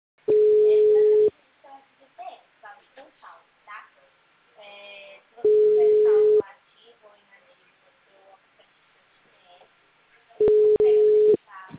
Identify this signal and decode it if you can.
Ringback tone (Europe, 425 Hz), with speech.
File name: out- -2037- - - .481